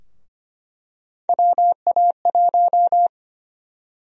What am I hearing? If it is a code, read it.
Morse code, heard as a keyed tone, about 25 wpm, 700 Hz. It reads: WA1
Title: USA Call Sign Prefixes - 25wpm